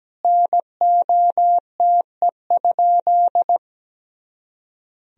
Morse code, heard as a keyed tone, about 17 wpm, 700 Hz. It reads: NOTE?
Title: Sentences from Top 500 Words and Abbreviations 17wpm